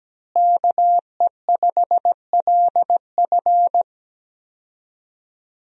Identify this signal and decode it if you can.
Morse code, heard as a keyed tone, about 17 wpm, 700 Hz. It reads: KE5LF
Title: USA Call Signs - 5 Characters - Repeated 2x - 17wpm